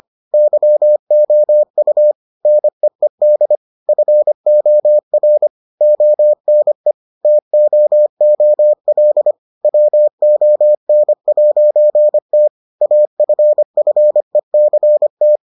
Morse code, heard as a keyed tone, about 25 wpm, 600 Hz. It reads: YOU NEED FOR ONE TOOL WON'T AFFECT